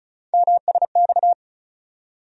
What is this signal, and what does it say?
Morse code, heard as a keyed tone, about 36 wpm, 700 Hz. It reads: MSX
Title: USA Call Sign Postfixes - 3 Letters 35wpm